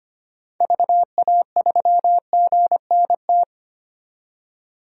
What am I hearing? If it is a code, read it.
Morse code, heard as a keyed tone, about 25 wpm, 690 Hz. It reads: VA3GNT
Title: DXCC-100 Call Signs - 6 Characters - 25wpm